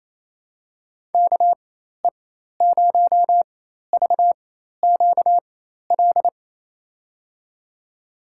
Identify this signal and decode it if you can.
Morse code, heard as a keyed tone, about 28 wpm, spaced as for 13 wpm, 690 Hz. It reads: KE0VQL
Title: USA Call Signs - 6 Characters - 4x Character Spacing - 28wpm